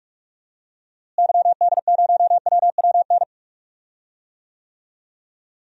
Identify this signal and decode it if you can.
Morse code, heard as a keyed tone, about 43 wpm, 700 Hz. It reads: YD0WWN
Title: DXCC-100 Call Signs - 6 Characters - Repeated 3x - 45wpm